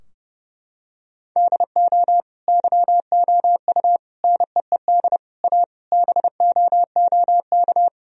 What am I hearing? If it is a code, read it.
Morse code, heard as a keyed tone, about 30 wpm, 700 Hz. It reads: DO YOU NEED A BOOK
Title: Sentences from Top 1000 Words - Review Part 4 - 30wpm